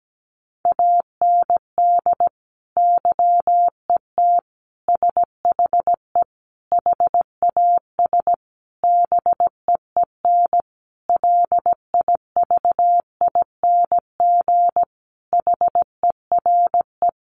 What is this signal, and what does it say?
Morse code, heard as a keyed tone, about 17 wpm, 700 Hz. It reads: AND YET SHE HAS BEEN LIVING HERE
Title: Sentences from Top 1100 Words 17wpm